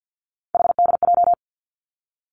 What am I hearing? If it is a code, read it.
Morse code, heard as a keyed tone, about 50 wpm, 700 Hz. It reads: HDJ
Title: Three Challenging Letter-Number Combinations 50wpm